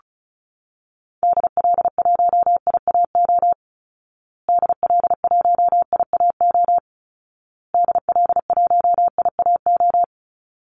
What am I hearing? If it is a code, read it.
Morse code, heard as a keyed tone, about 34 wpm, 700 Hz. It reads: DL1IAO DL1IAO DL1IAO
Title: Call Signs Repeated 2x - 35wpm